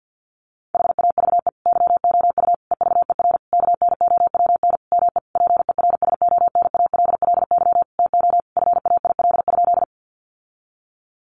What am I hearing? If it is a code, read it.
Morse code, heard as a keyed tone, about 50 wpm, 700 Hz. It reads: HAVE YOU EVER KNOWN ME PERSONALLY TO FAIL?
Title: Sentences from Top 1400 Words 50wpm